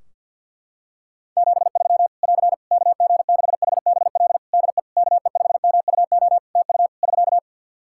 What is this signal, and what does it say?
Morse code, heard as a keyed tone, about 51 wpm, 700 Hz. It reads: 73 <AR> KG6HBL DE KE5MUO TU <SK>
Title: Practice QSO - Exchange 4 - Ending - 50wpm